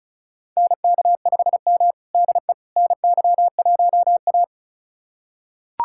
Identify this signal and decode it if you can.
Morse code, heard as a keyed tone, about 34 wpm, 700 Hz. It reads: NK5M DE NY1A
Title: USA Callsign DE Callsign - 4 Characters - 35wpm